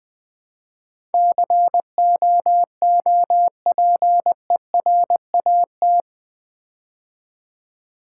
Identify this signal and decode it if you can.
Morse code, heard as a keyed tone, about 20 wpm, 700 Hz. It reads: COOPERAT
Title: Build A Word - 4 Letters At A Time - 11-Letter Words - 20wpm